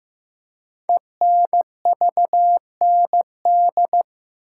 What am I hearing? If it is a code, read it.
Morse code, heard as a keyed tone, about 15 wpm, 700 Hz. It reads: ENVND
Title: Code Groups - 5 Letters - 15wpm